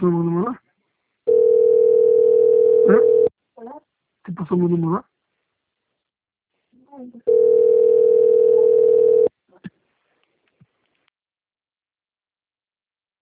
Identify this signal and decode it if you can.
Ringback tone (North America), 440 + 480 Hz, with speech.
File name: rg-1600-unknown- - - .62757